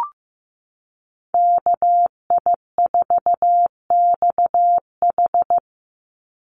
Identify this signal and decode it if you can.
Morse code, heard as a keyed tone, about 15 wpm, 700 Hz. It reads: KI4XH
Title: Call Signs Repeated 3x - 15wpm